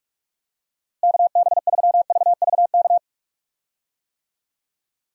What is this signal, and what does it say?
Morse code, heard as a keyed tone, about 46 wpm, 700 Hz. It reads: KB3VVK
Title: USA Call Signs - 6 Characters - Repeated 3x - 45wpm